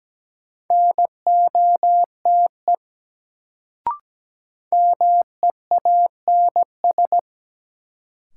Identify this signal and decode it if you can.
Morse code, heard as a keyed tone, about 17 wpm, 700 Hz. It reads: NOTE MEANS
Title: Sets of 5 Words - New Word Drill - 17wpm